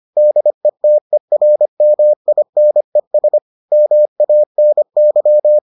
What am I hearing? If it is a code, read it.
Morse code, heard as a keyed tone, about 25 wpm, 600 Hz. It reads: DETERMINES MANY